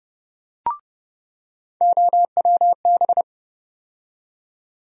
Morse code, heard as a keyed tone, about 30 wpm, 700 Hz. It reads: OWB
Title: USA Call Sign Postfixes - 3 Letters 30wpm